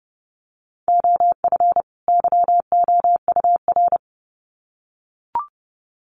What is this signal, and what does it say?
Morse code, heard as a keyed tone, about 30 wpm, 700 Hz. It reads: OF YOUR
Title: Sets of 2 Words 30wpm